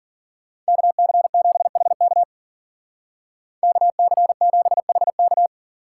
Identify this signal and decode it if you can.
Morse code, heard as a keyed tone, about 52 wpm, 700 Hz. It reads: KC7HK KC7HK